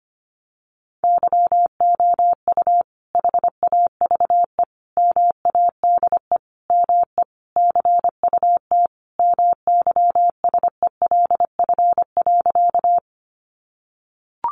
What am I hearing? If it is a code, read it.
Morse code, heard as a keyed tone, about 25 wpm, 700 Hz. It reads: YOU HAVE MADE ME CUT MYSELF.
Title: Sentences from Top 1300 Words 25wpm